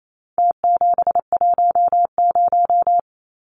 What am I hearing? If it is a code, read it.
Morse code, heard as a keyed tone, about 28 wpm, 700 Hz. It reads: T710
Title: Numbers - Contest Style - 4-Digit 28wpm